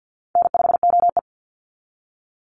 Morse code, heard as a keyed tone, about 50 wpm, 700 Hz. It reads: N5OE
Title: Call Signs - 50wpm